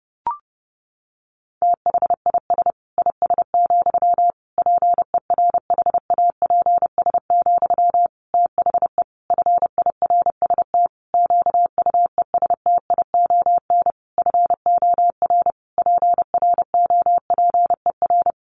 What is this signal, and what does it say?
Morse code, heard as a keyed tone, about 30 wpm, 700 Hz. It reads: THIS IS, PERHAPS, THE FIRST QUESTION FOR PROPER
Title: Sentences from Top 1500 Words 30wpm